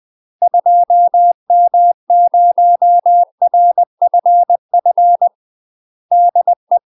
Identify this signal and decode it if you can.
Morse code, heard as a keyed tone, about 20 wpm, 700 Hz. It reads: 2M0RFF DE
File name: UK MASTER QSOs TRAINING / 20WPM / UK MASTER QSOs 0071